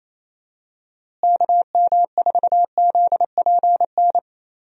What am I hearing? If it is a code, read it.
Morse code, heard as a keyed tone, about 28 wpm, 700 Hz. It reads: KM4ZPN
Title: USA Call Signs - 6 Characters - Repeated 2x - 28wpm